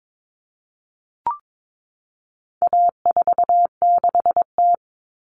Morse code, heard as a keyed tone, about 22 wpm, 700 Hz. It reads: A46T
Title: Common Cut Numbers - 4-Digit 22wpm